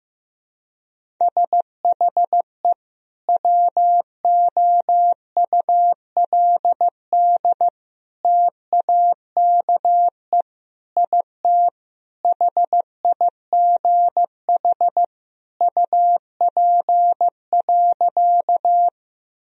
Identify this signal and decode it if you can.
Morse code, heard as a keyed tone, about 15 wpm, 700 Hz. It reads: SHE WOULD TAKE IT HIGH UP.
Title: Sentences from Top 1000 Words - Review Part 1 - 15wpm